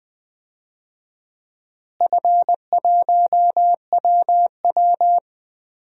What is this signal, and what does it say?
Morse code, heard as a keyed tone, about 20 wpm, 700 Hz. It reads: F1WW
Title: DXCC-100 Call Signs - 4 Characters - Repeated 2x - 20wpm